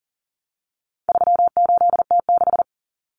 Morse code, heard as a keyed tone, about 40 wpm, 700 Hz. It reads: 38T6
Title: Common Cut Numbers - 4-Digit 40wpm